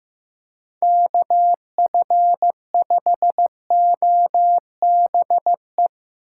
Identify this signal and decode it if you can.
Morse code, heard as a keyed tone, about 15 wpm, 700 Hz. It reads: KF5OBE
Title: USA Call Signs - 6x Character Spacing - 15wpm